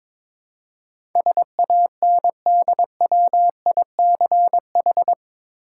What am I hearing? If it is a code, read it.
Morse code, heard as a keyed tone, about 22 wpm, 700 Hz. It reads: SANDWICH
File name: practice / 22wpm / Eight Letter Words - 22wpm